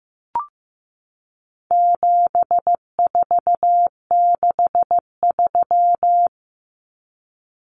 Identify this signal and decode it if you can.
Morse code, heard as a keyed tone, about 15 wpm, 700 Hz. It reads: 7463
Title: Common Cut Numbers - 4-Digit 15wpm